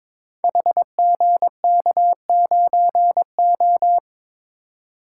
Morse code, heard as a keyed tone, about 22 wpm, 700 Hz. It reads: HGK9O
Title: Code Groups - 5 Letter:Numbers - 22wpm